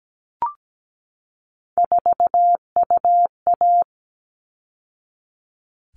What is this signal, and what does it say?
Morse code, heard as a keyed tone, about 17 wpm, 700 Hz. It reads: 4UA
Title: All Cut Numbers - 3-Digit 17wpm